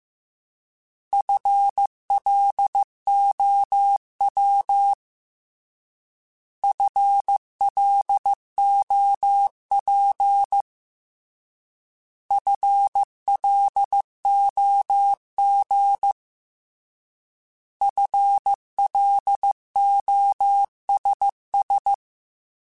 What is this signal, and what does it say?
Morse code, heard as a keyed tone, about 15 wpm, 800 Hz. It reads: FLOW FLOP FLOG FLOSS
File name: Alphabetical / Alphabetical mp3s / Fs @13